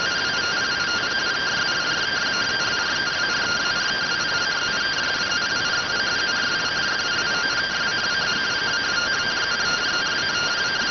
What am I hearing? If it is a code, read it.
Two-tone teleprinter signal, about 170 Hz shift.